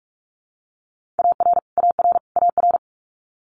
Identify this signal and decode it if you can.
Morse code, heard as a keyed tone, about 44 wpm, 700 Hz. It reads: AR AR AR
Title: US States Repeated 3x 45wpm